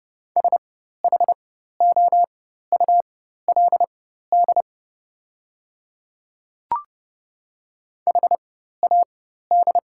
Morse code, heard as a keyed tone, about 30 wpm, spaced as for 14 wpm, 700 Hz. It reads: SHOULD HAD
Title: 4x Character Spacing - Top 500 Words - 30wpm